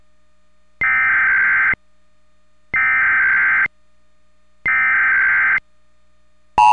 Emergency Alert System (SAME) tones (header bursts), heard as harsh buzzing bursts. It is heard from 0.8 to 5.6 s.